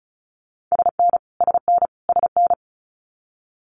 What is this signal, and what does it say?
Morse code, heard as a keyed tone, about 36 wpm, 700 Hz. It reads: SN SN SN
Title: Multi-letter Phonemes Repeated 3x 35wpm